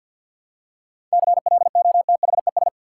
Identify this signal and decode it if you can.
Morse code, heard as a keyed tone, about 45 wpm, 700 Hz. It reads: CLOTHES
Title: Seven Letter Words - 50wpm